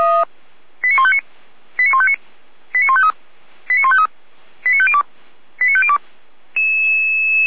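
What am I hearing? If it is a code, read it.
Five-tone selective call (ZVEI), heard as quick runs of five tones, tones of about 70 ms. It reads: 89159 89159 89153 89153 89502 89502
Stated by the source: ZVEI five-tone selective call 89099 received on an SDR, from the Feuerwehr Orlamünde audio archive.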